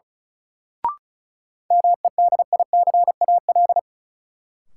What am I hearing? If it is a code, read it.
Morse code, heard as a keyed tone, about 36 wpm, 700 Hz. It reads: MEDICAL
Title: Sentences from Top 800 Words - New Word Drill - 35wpm